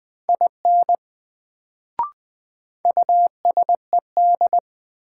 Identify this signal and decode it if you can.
Morse code, heard as a keyed tone, about 20 wpm, 700 Hz. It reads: IN USED
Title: Sets of 2 Words - New Word Drill - 20wpm